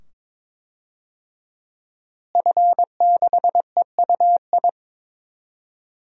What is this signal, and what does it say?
Morse code, heard as a keyed tone, about 22 wpm, 700 Hz. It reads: F6EUI
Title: DXCC-100 Call Signs - 5 Characters - Repeated 2x - 22wpm